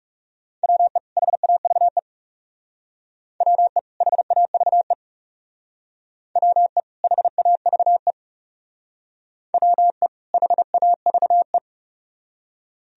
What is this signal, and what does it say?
Morse code, heard as a keyed tone, about 37 wpm, 700 Hz. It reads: WE HAVE WE HAVE WE HAVE WE HAVE
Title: Speed-Racing - Sets of 2 Words 30wpm